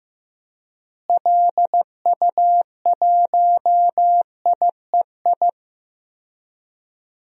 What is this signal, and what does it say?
Morse code, heard as a keyed tone, about 15 wpm, 700 Hz. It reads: LU1IEI
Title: DXCC-100 Call Signs - 6 Characters - Repeated 2x - 15wpm